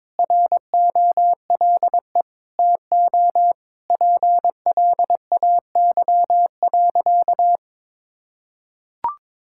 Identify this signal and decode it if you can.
Morse code, heard as a keyed tone, about 22 wpm, 700 Hz. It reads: ROLE TO PLAY.
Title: Sentences from Top 1000 Words - Review Part 6 - 22wpm